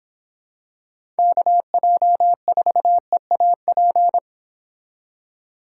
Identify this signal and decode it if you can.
Morse code, heard as a keyed tone, about 26 wpm, 700 Hz. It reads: KJ4EAP